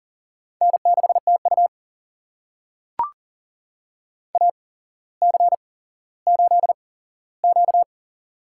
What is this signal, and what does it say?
Morse code, heard as a keyed tone, about 40 wpm, spaced as for 17 wpm, 700 Hz. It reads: N6TU AC8Q
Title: USA Call Signs - 4 Characters - 8x Character Spacing - 40wpm